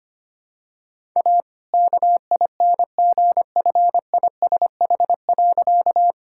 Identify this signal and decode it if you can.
Morse code, heard as a keyed tone, about 25 wpm, 700 Hz. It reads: A KINGFISH.
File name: practice / 25wpm / All-Age Appropriate Jokes - 25wpm